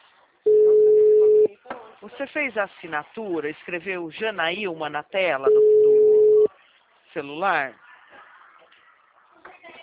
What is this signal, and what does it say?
Ringback tone (Europe, 425 Hz), with speech.